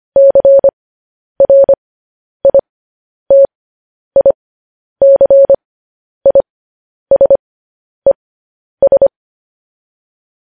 Morse code, heard as a keyed tone, about 25 wpm, spaced as for 10 wpm, 550 Hz. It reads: CRITICISES
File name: podcast / News headlines-1x-25x10